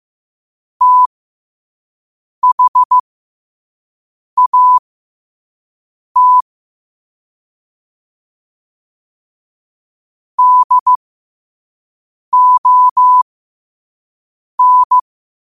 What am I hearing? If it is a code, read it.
Morse code, heard as a keyed tone, about 15 wpm, spaced as for 5 wpm, 1000 Hz. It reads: THAT DON